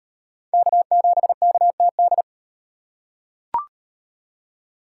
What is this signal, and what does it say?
Morse code, heard as a keyed tone, about 38 wpm, 700 Hz. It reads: K7KTD